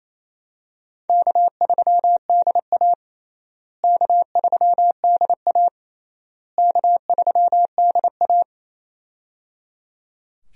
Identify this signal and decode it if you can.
Morse code, heard as a keyed tone, about 27 wpm, 700 Hz. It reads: K3DA K3DA K3DA
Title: USA Call Signs - 4 Characters - Repeated 3x - 28wpm